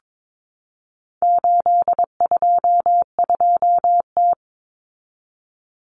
Morse code, heard as a keyed tone, about 22 wpm, 700 Hz. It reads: 822T
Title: Common Cut Numbers - 4-Digit 22wpm